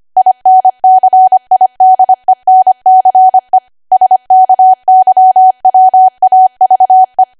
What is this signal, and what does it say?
Morse code, heard as a keyed tone, about 25 wpm, 750 Hz. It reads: INCIDENCE SKYWAVE
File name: Archive / 25 WPM / 25WPM